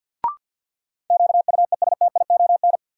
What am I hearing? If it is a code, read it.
Morse code, heard as a keyed tone, about 46 wpm, 700 Hz. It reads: QUESTION
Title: Sentences from Top 400 Words - New Word Drill - 50wpm